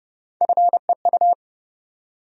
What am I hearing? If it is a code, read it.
Morse code, heard as a keyed tone, about 30 wpm, 700 Hz. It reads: FEU
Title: USA Call Sign Postfixes - 3 Letters 30wpm